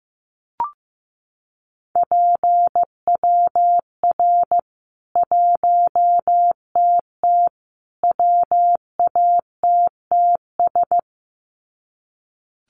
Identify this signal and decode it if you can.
Morse code, heard as a keyed tone, about 15 wpm, 700 Hz. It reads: PWR 1TT WATTS
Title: Common QSO Phrases 15wpm